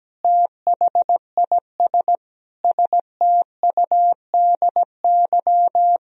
Morse code, heard as a keyed tone, about 17 wpm, 700 Hz. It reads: THIS STUDY
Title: Speed-Racing - Sets of 5 Words 17wpm